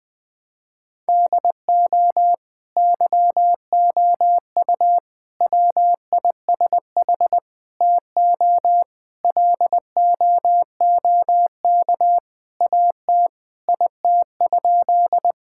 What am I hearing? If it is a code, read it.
Morse code, heard as a keyed tone, about 21 wpm, 700 Hz. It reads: DO YOU WISH TO LOOK AT IT?
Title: Sentences from Top 1600 Words 20wpm